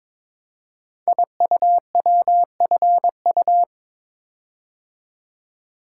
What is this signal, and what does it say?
Morse code, heard as a keyed tone, about 22 wpm, 690 Hz. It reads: IUWFU
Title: Code Groups - 5 Letters - 22wpm